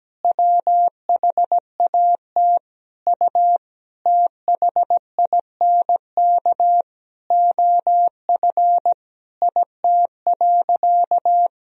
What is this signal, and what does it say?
Morse code, heard as a keyed tone, about 17 wpm, 700 Hz. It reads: WHAT U THINK OF IT.
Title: Sentences from Top 200 Words and Abbreviations 17wpm